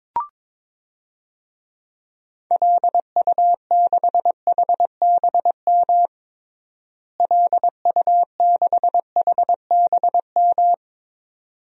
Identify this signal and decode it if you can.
Morse code, heard as a keyed tone, about 22 wpm, 700 Hz. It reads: LU6HBM LU6HBM
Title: DXCC-100 Call Signs - 6 Characters - Repeated 2x - 22wpm